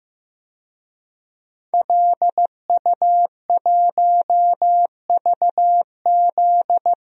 Morse code, heard as a keyed tone, about 15 wpm, 700 Hz. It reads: LU1VZ
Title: DXCC-100 Call Signs - 5 Characters - 15wpm